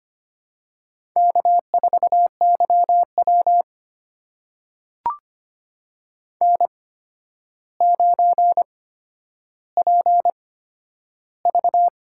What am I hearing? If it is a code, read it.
Morse code, heard as a keyed tone, about 25 wpm, spaced as for 11 wpm, 700 Hz. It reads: K4YW N9PV
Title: USA Call Signs - 4 Characters - 8x Character Spacing - 25wpm